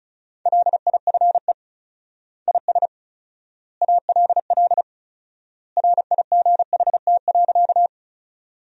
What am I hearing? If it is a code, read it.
Morse code, heard as a keyed tone, about 34 wpm, 700 Hz. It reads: LIFE IS ALL RIGHT.
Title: 4x Word-Spacing - Sentences from Top 300 Words - 35wpm